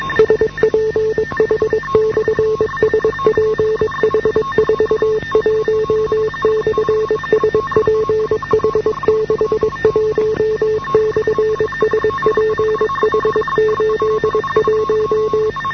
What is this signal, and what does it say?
Morse code, heard as a keyed tone, about 22 wpm, 430 Hz, with noise. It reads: SPH/SPH41/SPH61/SPH81